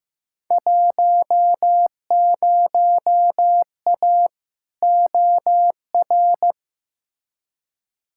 Morse code, heard as a keyed tone, about 15 wpm, 700 Hz. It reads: 10A OR
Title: Field Day Exchanges - 15wpm